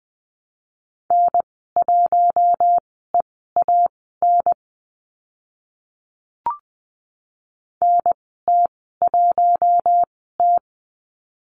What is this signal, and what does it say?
Morse code, heard as a keyed tone, about 20 wpm, spaced as for 15 wpm, 700 Hz. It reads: N1EAN NT1T
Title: 042 Lesson 8g - Number 1 - Call Signs 20wpm